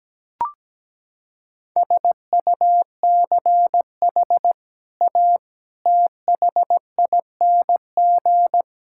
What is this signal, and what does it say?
Morse code, heard as a keyed tone, about 17 wpm, 700 Hz. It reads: SUCH A THING
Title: Sentences from Top 1000 Words - Review Part 6 - 17wpm